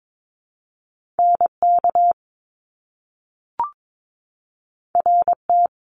Morse code, heard as a keyed tone, about 22 wpm, 700 Hz. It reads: NK RT
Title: Top Two Letter Combinations in Words 22wpm